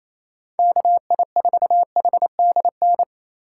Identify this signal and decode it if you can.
Morse code, heard as a keyed tone, about 28 wpm, 700 Hz. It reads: KI4HDN
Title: USA Call Signs - 6 Characters - 3x Character Spacing - 28wpm